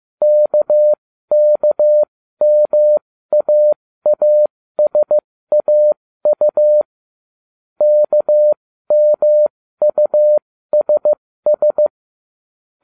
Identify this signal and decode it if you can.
Morse code, heard as a keyed tone, about 15 wpm, spaced as for 13 wpm, 600 Hz. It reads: KKMAASAU KMUSS